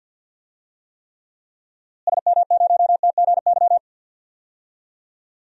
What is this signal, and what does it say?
Morse code, heard as a keyed tone, about 46 wpm, 700 Hz. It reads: SM0TGY
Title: DXCC-100 Call Signs - 6 Characters - Repeated 2x - 50wpm